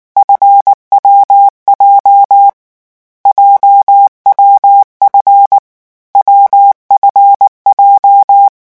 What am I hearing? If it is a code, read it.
Morse code, heard as a keyed tone, about 19 wpm, 800 Hz. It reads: FWJ JWF WFJ
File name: morse mp3 / lecon7-vitesse 18